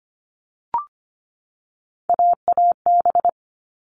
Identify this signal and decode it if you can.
Morse code, heard as a keyed tone, about 25 wpm, 700 Hz. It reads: AAB
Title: All Cut Numbers - 3-Digit 25wpm